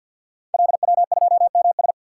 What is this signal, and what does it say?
Morse code, heard as a keyed tone, about 48 wpm, 700 Hz. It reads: LW2MS
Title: Code Groups - 5 Letter:Numbers - 50wpm